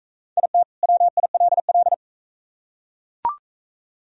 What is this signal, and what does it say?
Morse code, heard as a keyed tone, about 38 wpm, 700 Hz. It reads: IT WILL